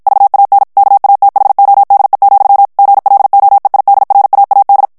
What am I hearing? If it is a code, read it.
Morse code, heard as a keyed tone, about 56 wpm, 800 Hz. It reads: VAN KATHODE, GLOEIDRAAD
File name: morse mp3 / HOLLANDAIS texte clair 2 vitesse 50 mots minute